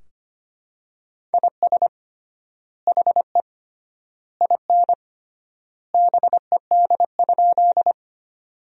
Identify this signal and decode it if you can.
Morse code, heard as a keyed tone, about 25 wpm, 700 Hz. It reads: IS HE IN BED?
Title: 3x Word-Spacing - Sentences from Top 500 Words - 25wpm